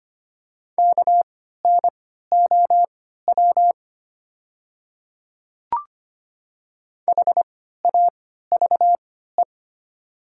Morse code, heard as a keyed tone, about 25 wpm, spaced as for 14 wpm, 700 Hz. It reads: KNOW HAVE
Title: 3x Character Spacing - Top 100 Words - 25wpm